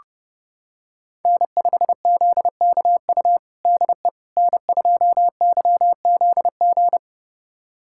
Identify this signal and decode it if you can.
Morse code, heard as a keyed tone, about 30 wpm, 700 Hz. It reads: N5ZKU DE N2YZG
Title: USA Callsign DE Callsign - 5 Characters - 30wpm